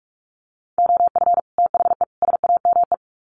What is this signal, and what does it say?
Morse code, heard as a keyed tone, about 44 wpm, 700 Hz. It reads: OF THE SAME
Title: Sets of 3 Words 45wpm